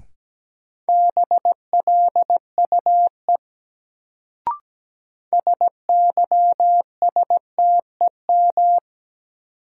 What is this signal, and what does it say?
Morse code, heard as a keyed tone, about 17 wpm, 700 Hz. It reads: BLUE SYSTEM
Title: Sentences from Top 1000 Words - Review Part 3 - New Word Drill - 17wpm